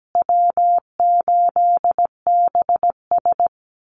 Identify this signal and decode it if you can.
Morse code, heard as a keyed tone, about 17 wpm, 700 Hz. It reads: W8BS
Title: USA Call Signs Repeated 2x 17wpm